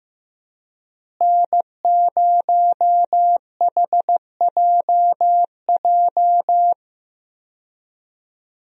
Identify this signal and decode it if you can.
Morse code, heard as a keyed tone, about 16 wpm, 700 Hz. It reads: N0HJJ
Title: USA Call Signs - 5 Characters - Repeated 2x - 15wpm